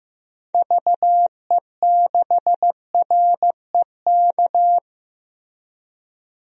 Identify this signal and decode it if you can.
Morse code, heard as a keyed tone, about 15 wpm, 700 Hz. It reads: VE6REK
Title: DXCC-100 Call Signs - 6 Characters - 15wpm